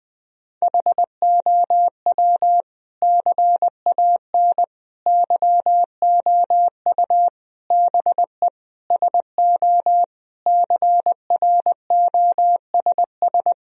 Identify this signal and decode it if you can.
Morse code, heard as a keyed tone, about 20 wpm, 700 Hz. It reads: HOW CAN YOU BE SO CROSS
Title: Sentences from Top 1000 Words - Review Part 1 - 20wpm